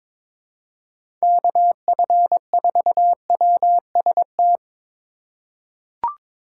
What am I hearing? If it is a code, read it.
Morse code, heard as a keyed tone, about 21 wpm, 700 Hz. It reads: KF4WST
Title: USA Call Signs - 6 Characters - 4x Character Spacing - 22wpm